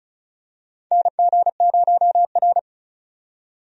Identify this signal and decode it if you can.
Morse code, heard as a keyed tone, about 36 wpm, 700 Hz. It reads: NG0R
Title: USA Call Signs - 4 Characters - 35wpm